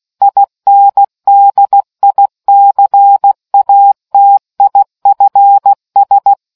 Morse code, heard as a keyed tone, about 16 wpm, 800 Hz. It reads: INDICATIFS